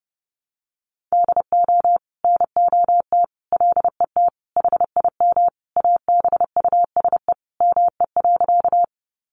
Morse code, heard as a keyed tone, about 28 wpm, 700 Hz. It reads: DO NOT LET HIM ABUSE ME.
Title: Sentences from Top 1300 Words 30wpm